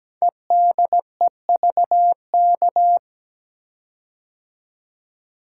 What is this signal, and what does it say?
Morse code, heard as a keyed tone, about 17 wpm, 700 Hz. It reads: EDEVK
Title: Code Groups - 5 Letters - 17wpm